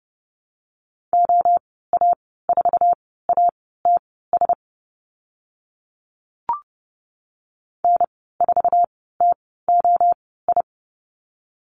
Morse code, heard as a keyed tone, about 30 wpm, spaced as for 17 wpm, 700 Hz. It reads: OA4ATS N4TOI
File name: Lesson 14g - Number 2 - Call Signs 30wpm